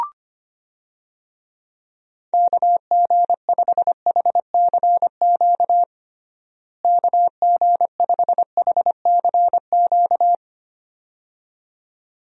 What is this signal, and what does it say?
Morse code, heard as a keyed tone, about 25 wpm, 700 Hz. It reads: KG5HCQ KG5HCQ
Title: USA Call Signs - 6 Characters - Repeated 2x - 25wpm